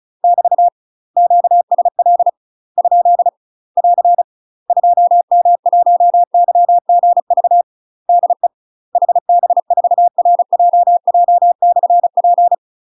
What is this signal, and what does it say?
Morse code, heard as a keyed tone, about 36 wpm, 700 Hz. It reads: <BT> QSL ? <AR> 2M1YGV DE HB4RJJ/P
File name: UK MASTER QSOs TRAINING / 35WPM / UK MASTER QSOs 0028